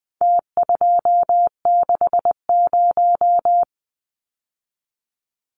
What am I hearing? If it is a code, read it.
Morse code, heard as a keyed tone, about 20 wpm, 700 Hz. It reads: T260
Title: Numbers - Contest Style - 4-Digit 20wpm